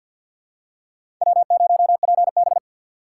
Morse code, heard as a keyed tone, about 52 wpm, 690 Hz. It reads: W0PB
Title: USA Call Signs - 3x Character Spacing - 50wpm